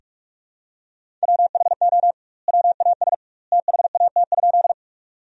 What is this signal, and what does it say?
Morse code, heard as a keyed tone, about 46 wpm, 700 Hz. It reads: WHO WAS THAT?
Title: 2x Word-Spacing - Sentences from Top 100 Words - 45wpm